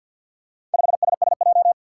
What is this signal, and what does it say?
Morse code, heard as a keyed tone, about 47 wpm, 700 Hz. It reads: 5SSJ
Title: Four Challenging Letter-Number Combinations 50wpm